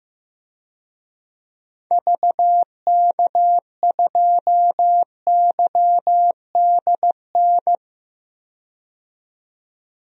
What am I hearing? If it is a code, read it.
Morse code, heard as a keyed tone, about 15 wpm, 700 Hz. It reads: VK2YDN
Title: DXCC-100 Call Signs - 6 Characters - 15wpm